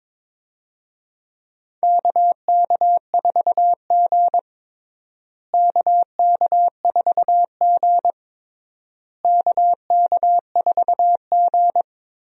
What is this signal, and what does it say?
Morse code, heard as a keyed tone, about 22 wpm, 700 Hz. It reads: KK4G KK4G KK4G
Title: USA Call Signs - 4 Characters - Repeated 3x - 22wpm